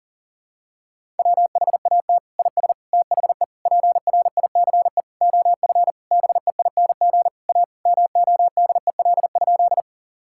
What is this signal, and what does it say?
Morse code, heard as a keyed tone, about 40 wpm, 700 Hz. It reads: WHAT IS THE PRICE OF BEING A MODEL?
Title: Sentences from Top 1000 Words - Review Part 5 - 40wpm